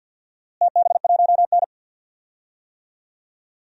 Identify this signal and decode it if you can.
Morse code, heard as a keyed tone, about 50 wpm, 700 Hz. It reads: TB1N
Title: DXCC-100 Call Signs - 4 Characters - Repeated 3x - 50wpm